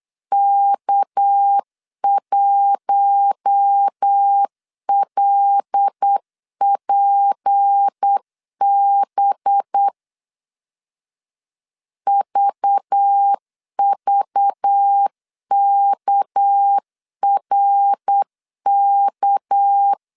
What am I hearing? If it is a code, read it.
Morse code, heard as a keyed tone, about 8 wpm, 800 Hz. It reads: K1LPB VVKRK
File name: CW / DET / 1 6 1500C S B 8wpm 5142